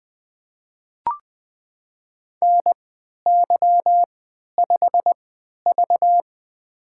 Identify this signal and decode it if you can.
Morse code, heard as a keyed tone, about 20 wpm, spaced as for 11 wpm, 700 Hz. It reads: NY5V
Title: USA Call Signs - 4 Characters - 3x Character Spacing - 20wpm